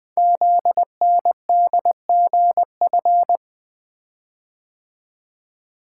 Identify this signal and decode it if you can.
Morse code, heard as a keyed tone, about 20 wpm, 700 Hz. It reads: ZNDGF
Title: Code Groups - 5 Letters - 20wpm